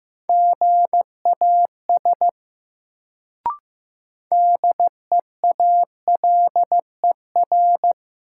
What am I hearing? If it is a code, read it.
Morse code, heard as a keyed tone, about 15 wpm, 700 Hz. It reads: GAS DEALER
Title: Sentences from Top 1100 Words - New Word Drill Alt - 15wpm